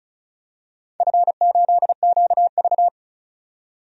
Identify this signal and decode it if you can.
Morse code, heard as a keyed tone, about 36 wpm, 700 Hz. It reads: F8QV
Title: DXCC-100 Call Signs - 4 Characters - 35wpm